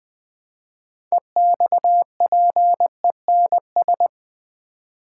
Morse code, heard as a keyed tone, about 20 wpm, 700 Hz. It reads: EXPENS
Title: Build A Word - 3 Letters At A Time - 9-Letter Words - 20wpm